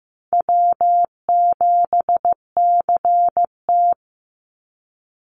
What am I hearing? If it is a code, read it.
Morse code, heard as a keyed tone, about 15 wpm, 700 Hz. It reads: W7CT
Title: Easy Call Signs Repeated 2x - 15wpm